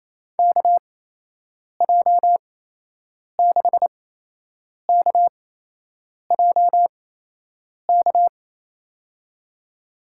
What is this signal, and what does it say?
Morse code, heard as a keyed tone, about 27 wpm, spaced as for 8 wpm, 700 Hz. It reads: KJ6KJK
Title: USA Call Signs - 6 Characters - 8x Character Spacing - 28wpm